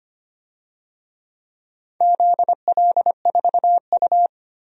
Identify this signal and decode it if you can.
Morse code, heard as a keyed tone, about 25 wpm, 700 Hz. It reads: ZL4U
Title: DXCC-100 Call Signs - 4 Characters - 25wpm